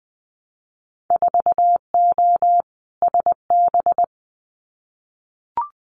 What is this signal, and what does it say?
Morse code, heard as a keyed tone, about 20 wpm, 700 Hz. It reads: 4O SB
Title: Winter Field Day Exchanges 20wpm